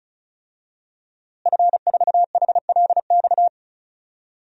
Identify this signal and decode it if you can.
Morse code, heard as a keyed tone, about 34 wpm, 700 Hz. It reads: F4HLX
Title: DXCC-100 Call Signs - 5 Characters - Repeated 2x - 35wpm